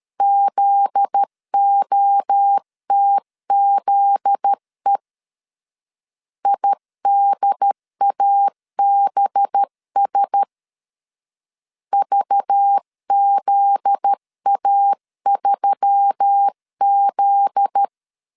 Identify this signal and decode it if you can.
Morse code, heard as a keyed tone, about 13 wpm, 800 Hz. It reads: ZOTZE IDABS VZA3Z
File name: LEZIONI / 7 GF9Z / 7 GF9Z 12wpm 2147